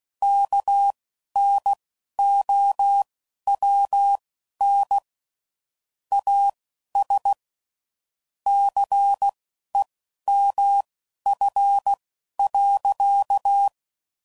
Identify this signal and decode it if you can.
Morse code, heard as a keyed tone, about 16 wpm, spaced as for 11 wpm, 800 Hz. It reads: KNOWN AS CEMF.
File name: CW Code Groups @10WPM / Technical Subjects / XL